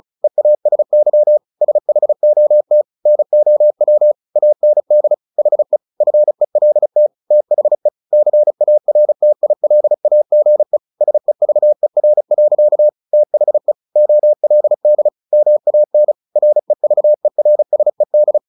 Morse code, heard as a keyed tone, about 34 wpm, 600 Hz. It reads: EASY SHOT NOW AND HE FELT THE CARTILAGE SEVER. THE OLD MAN REVERSED
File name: the old man and the sea 2 35wpm